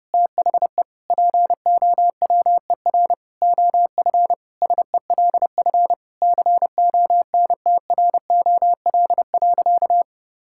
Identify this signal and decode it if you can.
Morse code, heard as a keyed tone, about 32 wpm, 700 Hz. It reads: THE POWER OF SELF CONTROL.
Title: Sentences from Top 1000 Words - Review Part 5 - 30wpm